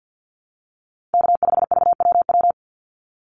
Morse code, heard as a keyed tone, about 50 wpm, 700 Hz. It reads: K5VWW
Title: Call Signs Repeated 2x - 50wpm